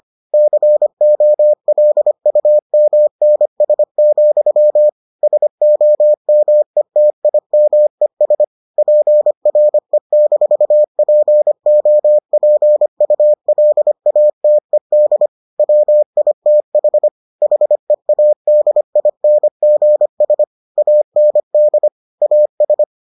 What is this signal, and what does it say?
Morse code, heard as a keyed tone, about 25 wpm, 600 Hz. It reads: COLUMNS, SOMETIMES PRE-POPULATED WITH HEADINGS AND AS